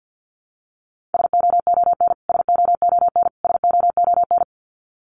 Morse code, heard as a keyed tone, about 50 wpm, 700 Hz. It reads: SOON SOON SOON
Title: Top 500 Words Repeated 50wpm